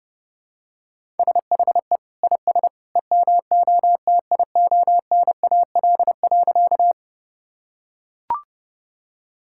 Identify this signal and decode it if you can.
Morse code, heard as a keyed tone, about 30 wpm, 700 Hz. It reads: SHE IS EMOTIONAL.
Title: Sentences from Top 1000 Words - Review Part 5 - 30wpm